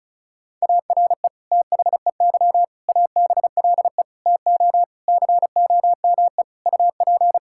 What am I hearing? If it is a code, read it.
Morse code, heard as a keyed tone, about 36 wpm, 690 Hz. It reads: ARE THEY ABLE TO COME UP
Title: Sentences from Top 1000 Words - Review Part 5 - 35wpm